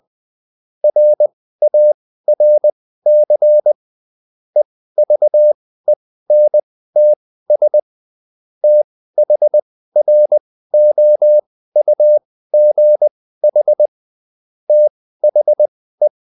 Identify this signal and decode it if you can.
Morse code, heard as a keyed tone, about 20 wpm, spaced as for 14 wpm, 600 Hz. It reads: RARC EVENTS THROUGH THE